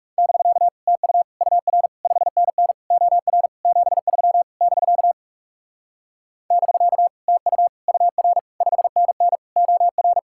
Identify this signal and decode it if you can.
Morse code, heard as a keyed tone, about 37 wpm, 700 Hz. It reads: <BK> TU UR 5NN OR 73 <BK> <BK> TU UR 5NN OR
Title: Speed-Racing Common QSO Phrases - 30wpm